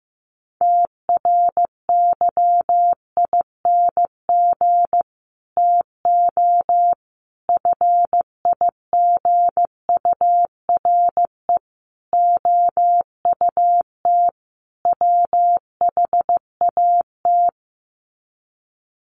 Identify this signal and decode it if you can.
Morse code, heard as a keyed tone, about 15 wpm, 700 Hz. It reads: TRYING TO FIGURE OUT WHAT
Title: Sets of 5 Words 15wpm